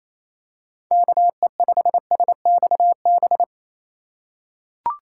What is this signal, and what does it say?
Morse code, heard as a keyed tone, about 28 wpm, 700 Hz. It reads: KE5SXB
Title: USA Call Signs - 6 Characters - 3x Character Spacing - 28wpm